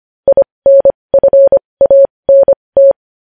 Morse code, heard as a keyed tone, about 25 wpm, spaced as for 20 wpm, 550 Hz. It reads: INFANT
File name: podcast / News headlines-1x-25x20